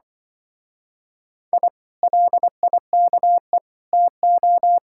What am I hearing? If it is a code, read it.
Morse code, heard as a keyed tone, about 25 wpm, 700 Hz. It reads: I LIKE TO